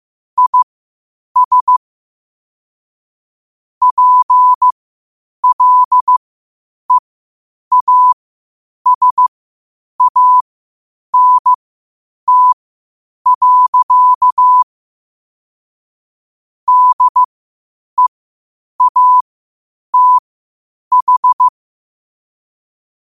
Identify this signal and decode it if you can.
Morse code, heard as a keyed tone, about 15 wpm, spaced as for 8 wpm, 1000 Hz. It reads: IS PLEASANT. DEATH
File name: qotdfiles / 8wpm